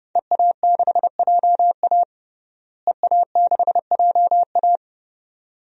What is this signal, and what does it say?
Morse code, heard as a keyed tone, about 30 wpm, 700 Hz. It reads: EA6JA EA6JA
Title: DXCC-100 Call Signs - 5 Characters - Repeated 2x - 30wpm